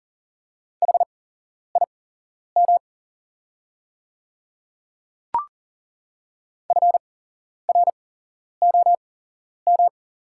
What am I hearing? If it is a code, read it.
Morse code, heard as a keyed tone, about 38 wpm, spaced as for 11 wpm, 700 Hz. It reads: HIM FROM
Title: 8x Character Spacing - Top 100 Words - 40wpm